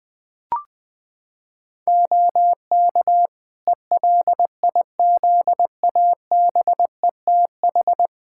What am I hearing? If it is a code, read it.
Morse code, heard as a keyed tone, about 20 wpm, 700 Hz. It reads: OK ELIZABETH
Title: Practice QSO - Exchange 3 - More Info - 20wpm